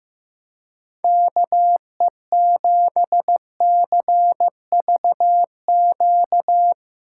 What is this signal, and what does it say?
Morse code, heard as a keyed tone, about 16 wpm, 700 Hz. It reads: KE7CVQ
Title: Speed Racing - USA 2x3 Call Signs 15wpm